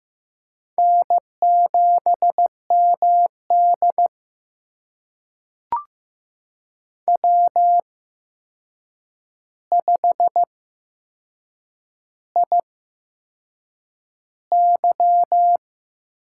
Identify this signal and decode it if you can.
Morse code, heard as a keyed tone, about 15 wpm, spaced as for 6 wpm, 700 Hz. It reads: N7MD W5IY
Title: USA Call Signs - 4 Characters - 8x Character Spacing - 15wpm